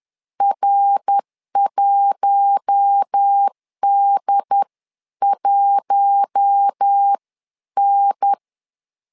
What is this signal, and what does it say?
Morse code, heard as a keyed tone, about 10 wpm, spaced as for 9 wpm, 800 Hz. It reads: R1D1N
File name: LEZIONI / 4 Q1 10wpm 2448